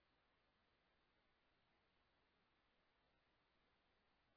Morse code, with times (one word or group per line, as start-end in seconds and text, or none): none